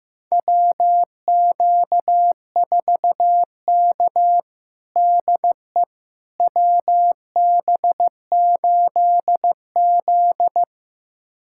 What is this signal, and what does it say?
Morse code, heard as a keyed tone, about 15 wpm, 700 Hz. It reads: WQ4K DE WB8Z
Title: USA Callsign DE Callsign - 4 Characters - 15wpm